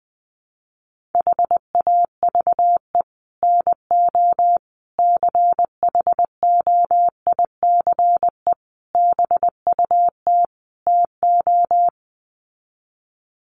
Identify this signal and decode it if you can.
Morse code, heard as a keyed tone, about 20 wpm, 700 Hz. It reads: HAVE NO CHOICE BUT TO
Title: Sets of 5 Words Encore 20wpm